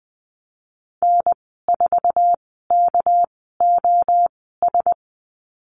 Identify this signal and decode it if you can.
Morse code, heard as a keyed tone, about 20 wpm, spaced as for 15 wpm, 700 Hz. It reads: N4KOS
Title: 238 Lesson 38g - Letter q - Call Signs 20wpm